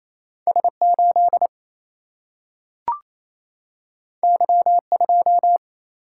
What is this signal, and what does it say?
Morse code, heard as a keyed tone, about 28 wpm, 700 Hz. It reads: S8 Y2
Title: Single-Letter Single-Number 28wpm